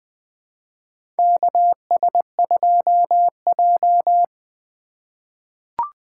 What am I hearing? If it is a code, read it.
Morse code, heard as a keyed tone, about 20 wpm, 700 Hz. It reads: KS2J
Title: USA Call Signs - 4 Characters - 2x Character Spacing - 20wpm